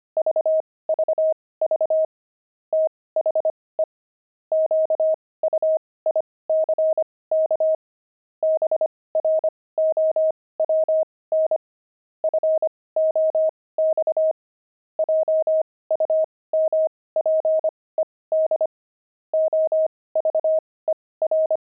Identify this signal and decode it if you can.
Morse code, heard as a keyed tone, about 25 wpm, spaced as for 18 wpm, 630 Hz. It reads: VVV THE QUICK BROWN FOX JUMPED OVER
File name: pangram-fs-18-cs-25-fox